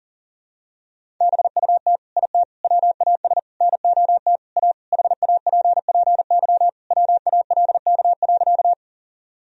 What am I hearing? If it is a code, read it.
Morse code, heard as a keyed tone, about 40 wpm, 700 Hz. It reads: BUT IT WAS NOT A HAPPY WALK.
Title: Sentences from Top 1000 Words - Review Part 1 - 40wpm